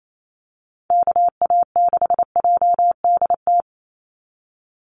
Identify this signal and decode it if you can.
Morse code, heard as a keyed tone, about 28 wpm, 700 Hz. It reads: KA6JDT
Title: USA Call Signs 28wpm